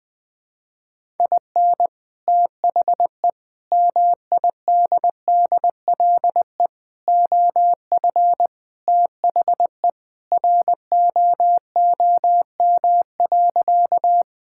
Morse code, heard as a keyed tone, about 20 wpm, 700 Hz. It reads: IN THE MIDDLE OF THE ROOM.
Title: Sentences with 2-Word Phrases - 20wpm